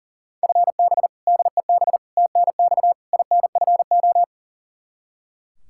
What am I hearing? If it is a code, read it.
Morse code, heard as a keyed tone, about 37 wpm, 700 Hz. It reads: FB DEB TNX INFO
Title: Practice QSO - Exchange 3 - More Info - 40wpm